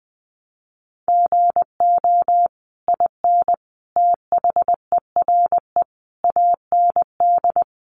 Morse code, heard as a keyed tone, about 20 wpm, 700 Hz. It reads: GO IN THERE AND
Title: Sentences from Top 500 Words 20wpm